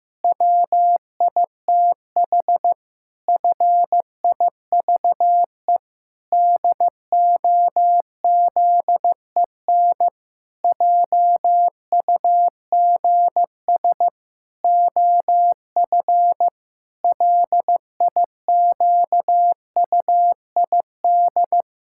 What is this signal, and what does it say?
Morse code, heard as a keyed tone, about 15 wpm, 700 Hz. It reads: WITH FIVE DOZEN JUGS OF LIQUID
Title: Pangram Challenge v2 - 15wpm